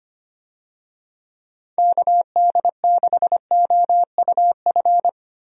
Morse code, heard as a keyed tone, about 25 wpm, 700 Hz. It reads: KD6OUF
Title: USA Call Signs - 6 Characters - Repeated 3x - 25wpm